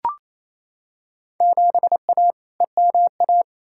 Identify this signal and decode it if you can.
Morse code, heard as a keyed tone, about 27 wpm, 700 Hz. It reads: 7A EMA
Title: Field Day Exchanges Encore - 28wpm